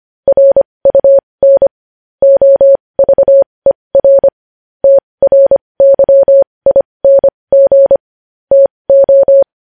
Morse code, heard as a keyed tone, about 25 wpm, spaced as for 20 wpm, 550 Hz. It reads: RUN OVER TRYING TO
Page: news headlines in morse code at 20 wpm